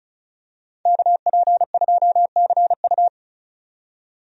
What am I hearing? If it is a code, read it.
Morse code, heard as a keyed tone, about 36 wpm, 700 Hz. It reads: KP2CU
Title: DXCC-100 Call Signs - 5 Characters - 35wpm